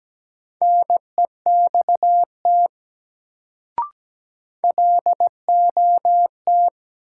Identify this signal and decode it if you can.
Morse code, heard as a keyed tone, about 17 wpm, 700 Hz. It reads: NEXT LOT
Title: Sets of 2 Words - New Word Drill - 17wpm